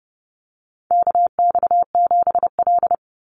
Morse code, heard as a keyed tone, about 30 wpm, 700 Hz. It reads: KX7L
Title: NA SOTA Chaser Callsigns 30wpm